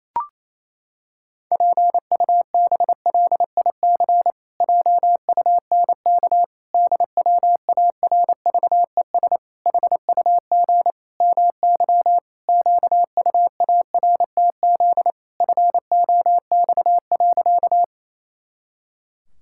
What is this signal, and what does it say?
Morse code, heard as a keyed tone, about 28 wpm, 700 Hz. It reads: PUBLIC JUNK DWARVES HUG MY QUARTZ FOX.
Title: Pangram Challenge v2 - 28wpm